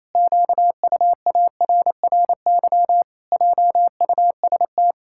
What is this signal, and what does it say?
Morse code, heard as a keyed tone, about 28 wpm, 700 Hz. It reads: QUARRY JUST
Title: Pangram Challenge v2 - 28wpm